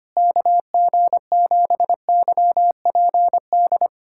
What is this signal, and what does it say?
Morse code, heard as a keyed tone, about 25 wpm, 700 Hz. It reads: KG7YPD
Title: USA Callsign DE Callsign 25wpm